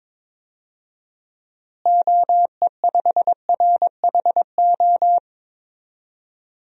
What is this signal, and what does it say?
Morse code, heard as a keyed tone, about 22 wpm, 700 Hz. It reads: OE5RHO
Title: DXCC-100 Call Signs - 6 Characters - Repeated 3x - 22wpm